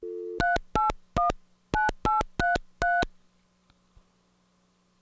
DTMF, heard as telephone touch tones, tones of about 160 ms. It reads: 3819833
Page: telephone-dialing-internal